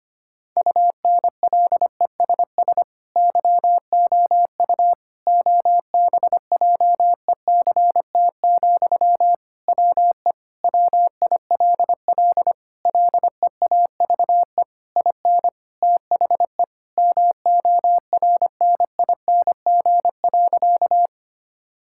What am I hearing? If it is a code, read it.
Morse code, heard as a keyed tone, about 25 wpm, 700 Hz. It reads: UNLESS YOU OBJECT, WE WILL LEAVE IN THE MORNING.
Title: Sentences from Top 1000 Words - Review Part 6 - 25wpm